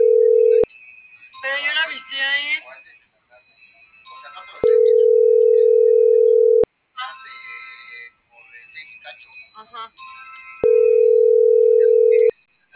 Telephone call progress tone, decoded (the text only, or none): ringback tone